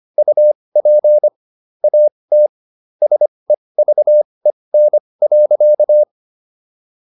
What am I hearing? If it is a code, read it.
Morse code, heard as a keyed tone, about 25 wpm, spaced as for 20 wpm, 600 Hz. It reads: UP AT SEVEN.